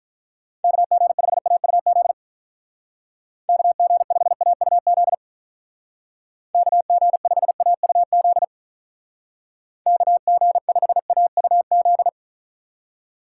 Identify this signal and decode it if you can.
Morse code, heard as a keyed tone, about 45 wpm, 700 Hz. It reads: KG5AUZ KG5AUZ KG5AUZ KG5AUZ